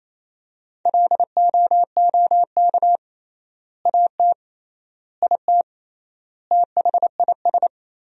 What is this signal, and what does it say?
Morse code, heard as a keyed tone, about 28 wpm, 700 Hz. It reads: LOOK AT IT THIS
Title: 3x Word-Spacing - Sentences from Top 100 Words - 28wpm